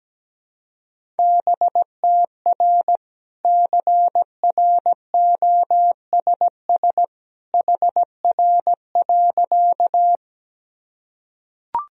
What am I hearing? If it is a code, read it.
Morse code, heard as a keyed tone, about 17 wpm, 700 Hz. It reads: BTR CROSS HR.
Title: Sentences from Top 400 Words and Abbreviations 17wpm